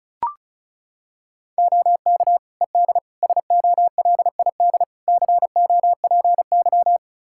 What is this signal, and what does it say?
Morse code, heard as a keyed tone, about 36 wpm, 700 Hz. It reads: OK ED SOLID COPY
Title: Practice QSO - Exchange 3 - More Info - 35wpm